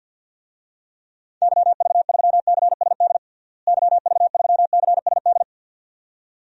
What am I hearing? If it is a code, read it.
Morse code, heard as a keyed tone, about 46 wpm, 700 Hz. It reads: YV3CSD YV3CSD
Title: DXCC-100 Call Signs - 6 Characters - Repeated 2x - 50wpm